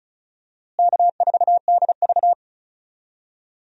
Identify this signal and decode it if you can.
Morse code, heard as a keyed tone, about 34 wpm, 700 Hz. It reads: K4DV
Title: USA Call Signs - 4 Characters - Repeated 2x - 35wpm